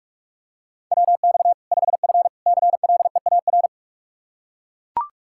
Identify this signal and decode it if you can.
Morse code, heard as a keyed tone, about 41 wpm, 700 Hz. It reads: WX 5F CLEAR
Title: Practice QSO - Exchange 2 - Info - 45wpm